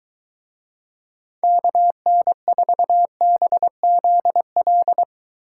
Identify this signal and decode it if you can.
Morse code, heard as a keyed tone, about 21 wpm, 700 Hz. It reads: KN4BZL